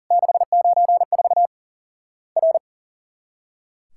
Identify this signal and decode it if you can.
Morse code, heard as a keyed tone, about 41 wpm, 700 Hz. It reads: 694 R
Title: Worked All Europe DX Contest - QTC Exchange - 40wpm